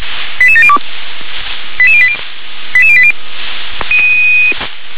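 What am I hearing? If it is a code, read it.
Five-tone selective call (ZVEI), heard as quick runs of five tones, tones of about 70 ms. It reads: 88702 88080 88080
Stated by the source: ZVEI five-tone selective call received on SDR.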